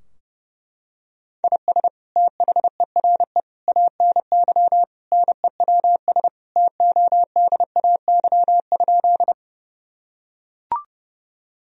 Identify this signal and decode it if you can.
Morse code, heard as a keyed tone, about 31 wpm, 700 Hz. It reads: IS THERE ANY NEWS TODAY?
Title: Sentences from Top 1000 Words - Review Part 4 - 30wpm